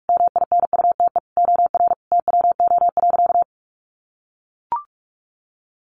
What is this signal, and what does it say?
Morse code, heard as a keyed tone, about 44 wpm, 700 Hz. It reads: MINUTE OR TWO.
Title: Sentences from Top 1400 Words 45wpm